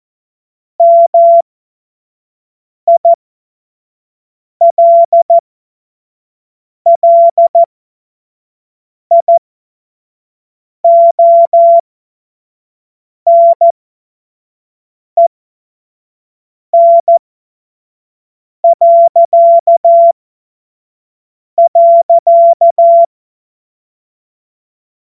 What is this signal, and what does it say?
Morse code, heard as a keyed tone, about 14 wpm, spaced as for 5 wpm, 680 Hz. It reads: MILLIONEN..
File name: Morsekurs / Witze / Witz 10 Juni 2025 25 F70